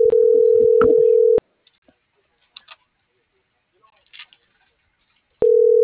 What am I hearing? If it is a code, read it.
Ringback tone (North America), 440 + 480 Hz.